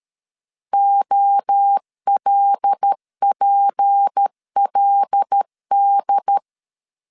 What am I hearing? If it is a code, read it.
Morse code, heard as a keyed tone, about 13 wpm, 800 Hz. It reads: OLPLD
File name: LEZIONI / 6 PKX0 / 6 PK 12wpm 2006